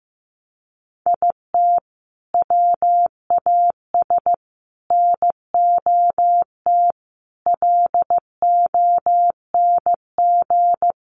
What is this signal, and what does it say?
Morse code, heard as a keyed tone, about 15 wpm, 700 Hz. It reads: IT WAS NOT LONG
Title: Sets of 5 Words Encore2 15wpm